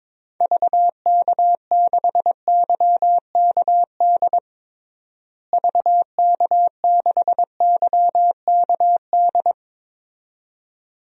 Morse code, heard as a keyed tone, about 22 wpm, 700 Hz. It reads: VK6YKD VK6YKD
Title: DXCC-100 Call Signs - 6 Characters - Repeated 2x - 22wpm